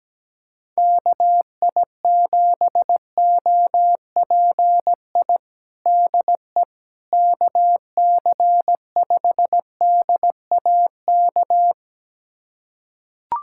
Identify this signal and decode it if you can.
Morse code, heard as a keyed tone, about 17 wpm, 700 Hz. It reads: KI7OPI DE KC5DAK
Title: USA Callsign DE Callsign 17wpm